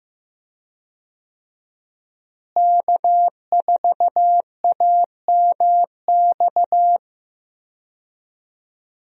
Morse code, heard as a keyed tone, about 15 wpm, 700 Hz. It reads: K4AMX
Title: USA Call Signs - 5 Characters - Repeated 2x - 15wpm